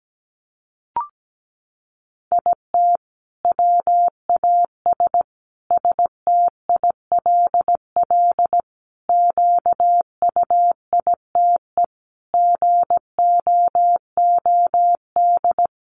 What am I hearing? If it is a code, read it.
Morse code, heard as a keyed tone, about 17 wpm, 700 Hz. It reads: IT WAS STILL QUITE GOOD
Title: Sentences from Top 1500 Words 17wpm